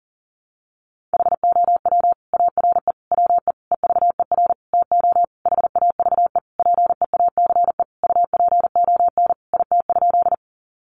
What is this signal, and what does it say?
Morse code, heard as a keyed tone, about 40 wpm, 700 Hz. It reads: HOW ARE WE EVER TO HAVE PEACE UPON IT?
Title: Sentences from Top 1500 Words 40wpm